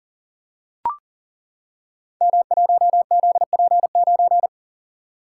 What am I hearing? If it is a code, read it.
Morse code, heard as a keyed tone, about 38 wpm, 700 Hz. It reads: M1ZP9
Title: Code Groups - 5 Letter:Numbers - 40wpm